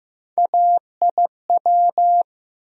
Morse code, heard as a keyed tone, about 15 wpm, 700 Hz. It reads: AIW
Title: USA Call Sign Postfixes - 3 Letters 15wpm